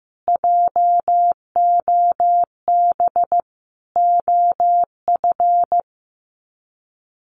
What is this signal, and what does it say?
Morse code, heard as a keyed tone, about 15 wpm, 700 Hz. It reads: JOB OF
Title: Sets of 5 Words Encore2 15wpm